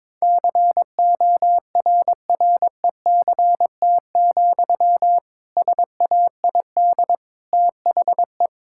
Morse code, heard as a keyed tone, about 22 wpm, 700 Hz. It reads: CORRECT, SAID THE
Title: Sentences from Top 1600 Words 22wpm